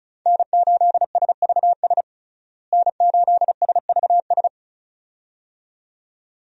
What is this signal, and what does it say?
Morse code, heard as a keyed tone, about 34 wpm, 700 Hz. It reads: N8SVS N8SVS
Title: USA Call Signs - 5 Characters - Repeated 2x - 35wpm